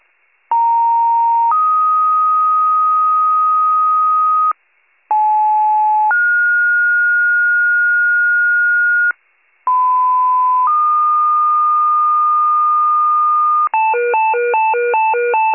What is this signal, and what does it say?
Two-tone page, heard as a one-second tone, then a three-second tone, about 930 then 1250 Hz, about 860 then 1430 Hz, about 1010 then 1220 Hz, with noise. It is heard from 0.5 to 13.7 s.